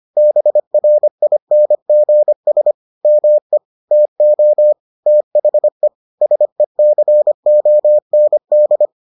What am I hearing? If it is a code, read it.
Morse code, heard as a keyed tone, about 25 wpm, 600 Hz. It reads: BRINGS ME TO THE SECOND